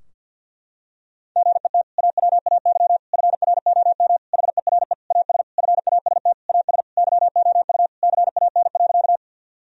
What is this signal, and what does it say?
Morse code, heard as a keyed tone, about 52 wpm, 700 Hz. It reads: GET AWAY FROM HERE AS FAST AS YOU CAN.
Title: Sentences from Top 1000 Words - Review Part 2 - 50wpm